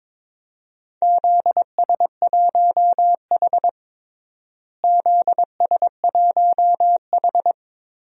Morse code, heard as a keyed tone, about 22 wpm, 700 Hz. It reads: ZS1H ZS1H
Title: DXCC-100 Call Signs - 4 Characters - Repeated 3x - 22wpm